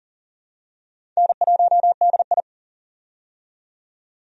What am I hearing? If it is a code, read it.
Morse code, heard as a keyed tone, about 40 wpm, 700 Hz. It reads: N1DI
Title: USA Call Signs - 4 Characters - 40wpm